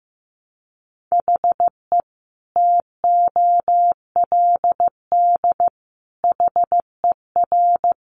Morse code, heard as a keyed tone, about 15 wpm, 700 Hz. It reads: HE TOLD HER
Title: Sentences from Top 700 Words 15wpm